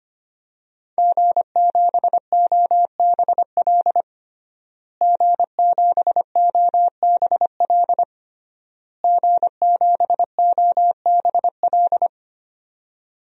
Morse code, heard as a keyed tone, about 25 wpm, 700 Hz. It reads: G7OBL G7OBL G7OBL
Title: DXCC-100 Call Signs - 5 Characters - Repeated 3x - 25wpm